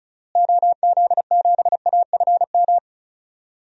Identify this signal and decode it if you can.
Morse code, heard as a keyed tone, about 36 wpm, 700 Hz. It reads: OZ7AFM
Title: DXCC-100 Call Signs - 6 Characters - Repeated 3x - 35wpm